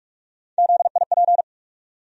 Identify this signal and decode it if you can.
Morse code, heard as a keyed tone, about 46 wpm, 700 Hz. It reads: ZIP
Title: USA Call Sign Postfixes - 3 Letters 45wpm